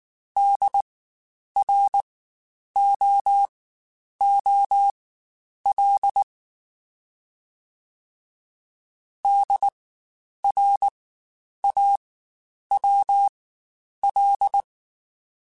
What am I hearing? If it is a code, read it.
Morse code, heard as a keyed tone, about 19 wpm, spaced as for 9 wpm, 800 Hz. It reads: DROOL DRAWL
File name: Spaced Alphas / Spaced Alphas mp3s / Ds sp @18